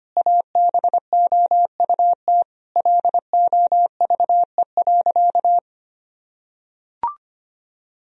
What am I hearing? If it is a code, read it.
Morse code, heard as a keyed tone, about 25 wpm, 700 Hz. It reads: ABOUT LOVE.
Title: Sentences from Top 1000 Words - Review Part 2 - 25wpm